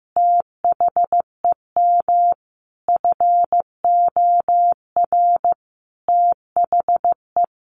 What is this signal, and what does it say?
Morse code, heard as a keyed tone, about 15 wpm, 700 Hz. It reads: THEM FOR THE
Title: Sentences from Top 1200 Words 15wpm